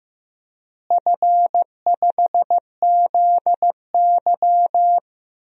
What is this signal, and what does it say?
Morse code, heard as a keyed tone, about 15 wpm, 700 Hz. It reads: F5ZY
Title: DXCC-100 Call Signs - 4 Characters - Repeated 2x - 15wpm